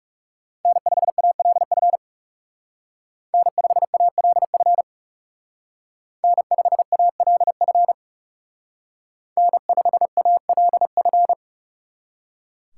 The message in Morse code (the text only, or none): N5ALF N5ALF N5ALF N5ALF